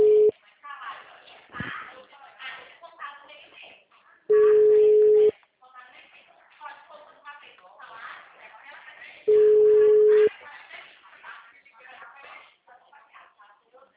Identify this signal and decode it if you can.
Ringback tone (Europe, 425 Hz), with speech.
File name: out- -2006- - - .2860